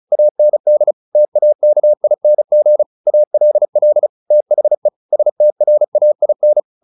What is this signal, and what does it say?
Morse code, heard as a keyed tone, about 34 wpm, 600 Hz. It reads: AND TAKING ALL THE STRAIN
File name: the old man and the sea / the old man and the sea 1 35wpm